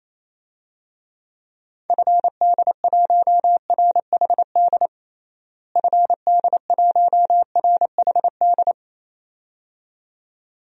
Morse code, heard as a keyed tone, about 28 wpm, 700 Hz. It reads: FD1RHD FD1RHD
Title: DXCC-100 Call Signs - 6 Characters - Repeated 2x - 28wpm